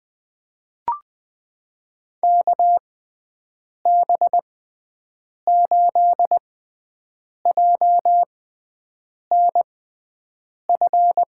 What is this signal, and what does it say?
Morse code, heard as a keyed tone, about 20 wpm, spaced as for 7 wpm, 700 Hz. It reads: KB8JNF
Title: USA Call Signs - 6 Characters - 6x Character Spacing - 20wpm